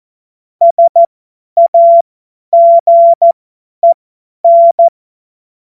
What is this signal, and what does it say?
Morse code, heard as a keyed tone, about 14 wpm, spaced as for 10 wpm, 680 Hz. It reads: SAGEN
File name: Morsekurs / Witze / Witz 21.November 2023 50 F70